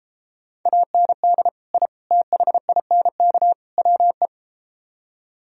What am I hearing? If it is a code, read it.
Morse code, heard as a keyed tone, about 33 wpm, 700 Hz. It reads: AND I THINK WE